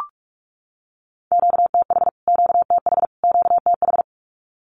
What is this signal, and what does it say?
Morse code, heard as a keyed tone, about 44 wpm, 700 Hz. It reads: QTH QTH QTH
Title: Q-Codes Repeated 3x 45wpm